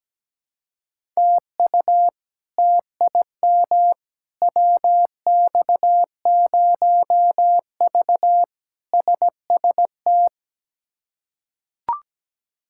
Morse code, heard as a keyed tone, about 17 wpm, 700 Hz. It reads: TU TIM WX0V SST
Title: K1USN SST QSO Practice - 17wpm